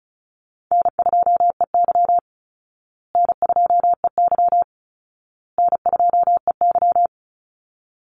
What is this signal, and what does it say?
Morse code, heard as a keyed tone, about 34 wpm, 700 Hz. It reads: N2EY N2EY N2EY
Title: Easy Call Signs Repeated 3x - 35wpm